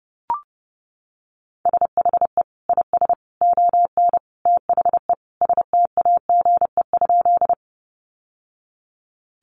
Morse code, heard as a keyed tone, about 30 wpm, 700 Hz. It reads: SHE IS ON THE STAGE?
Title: Sentences from Top 1200 Words 30wpm